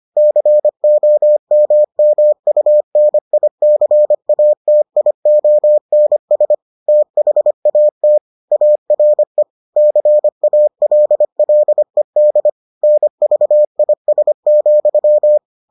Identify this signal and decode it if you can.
Morse code, heard as a keyed tone, about 25 wpm, 600 Hz. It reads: COMMUNICATIONS THAT ARE CALLED NVIS,